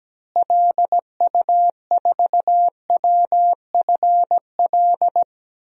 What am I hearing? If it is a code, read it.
Morse code, heard as a keyed tone, about 17 wpm, 700 Hz. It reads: LU4WFL
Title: DXCC-100 Call Signs - 6 Characters - Repeated 2x - 17wpm